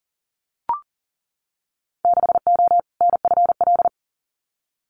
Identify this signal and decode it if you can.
Morse code, heard as a keyed tone, about 39 wpm, 700 Hz. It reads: 6O NFL
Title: Winter Field Day Exchanges 40wpm